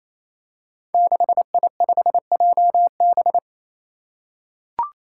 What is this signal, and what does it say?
Morse code, heard as a keyed tone, about 27 wpm, 700 Hz. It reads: 6I5JB
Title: Five Challenging Letter-Number Combinations 28wpm